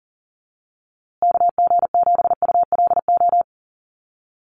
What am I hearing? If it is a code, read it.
Morse code, heard as a keyed tone, about 40 wpm, 700 Hz. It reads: KG7ULO
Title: USA Call Signs Repeated 3x 40wpm